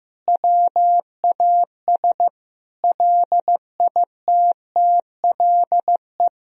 Morse code, heard as a keyed tone, about 15 wpm, 700 Hz. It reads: WAS LITTLE
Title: Sentences from Top 1000 Words - Review Part 2 - 15wpm